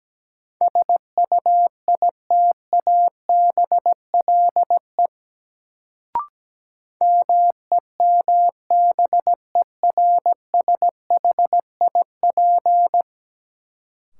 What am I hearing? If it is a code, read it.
Morse code, heard as a keyed tone, about 17 wpm, 700 Hz. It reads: SUITABLE MEMBERSHIP
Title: Sentences from Top 1400 Words - New Word Drill - 17wpm